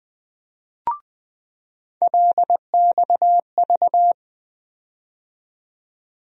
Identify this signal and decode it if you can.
Morse code, heard as a keyed tone, about 20 wpm, 700 Hz. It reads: LXV
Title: USA Call Sign Postfixes - 3 Letters 20wpm